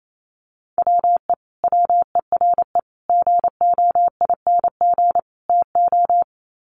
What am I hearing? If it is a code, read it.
Morse code, heard as a keyed tone, about 28 wpm, 700 Hz. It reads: WE WERE GOING TO
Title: Sets of 4 Words 28wpm